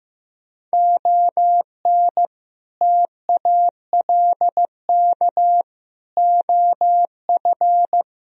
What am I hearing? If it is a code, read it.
Morse code, heard as a keyed tone, about 15 wpm, 700 Hz. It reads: ON TALK OF
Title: Speed-Racing - Sets of 5 Words 15wpm